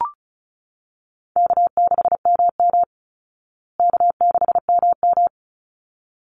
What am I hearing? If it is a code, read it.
Morse code, heard as a keyed tone, about 34 wpm, 700 Hz. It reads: K6MM K6MM
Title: Easy Call Signs Repeated 2x - 35wpm